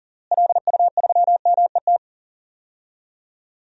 Morse code, heard as a keyed tone, about 40 wpm, 700 Hz. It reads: LU3MET
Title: DXCC-100 Call Signs - 6 Characters - Repeated 3x - 40wpm